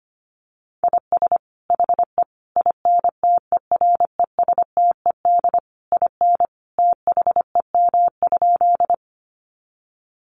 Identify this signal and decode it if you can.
Morse code, heard as a keyed tone, about 25 wpm, 700 Hz. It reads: IS HE INTERESTED IN THEM?
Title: Sentences from Top 900 Words 25wpm